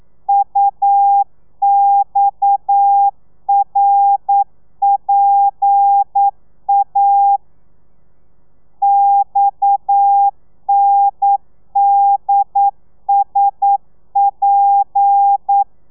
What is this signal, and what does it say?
Morse code, heard as a keyed tone, about 9 wpm, 800 Hz. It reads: UXRPA XNDSP
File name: telegrafia / i2tfj / Lez7 40